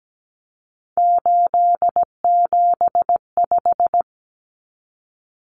Morse code, heard as a keyed tone, about 17 wpm, 700 Hz. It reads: 875
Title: Numbers - 3-Digit 17wpm